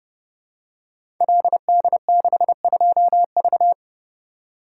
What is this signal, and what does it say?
Morse code, heard as a keyed tone, about 30 wpm, 700 Hz. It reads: LD62V
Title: Code Groups - 5 Letter:Numbers - 30wpm